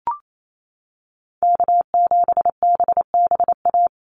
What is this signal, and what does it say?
Morse code, heard as a keyed tone, about 28 wpm, 700 Hz. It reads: K7BBA
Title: USA Call Signs 28wpm